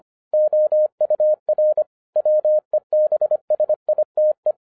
Morse code, heard as a keyed tone, about 25 wpm, 610 Hz. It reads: OUR WEBSITE